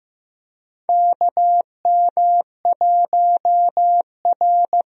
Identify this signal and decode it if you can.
Morse code, heard as a keyed tone, about 15 wpm, 700 Hz. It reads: KM1R
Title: USA Call Signs - 4 Characters - Repeated 2x - 15wpm